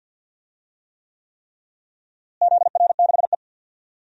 Morse code, heard as a keyed tone, about 52 wpm, 700 Hz. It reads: ZR6E
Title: DXCC-100 Call Signs - 4 Characters - Repeated 3x - 50wpm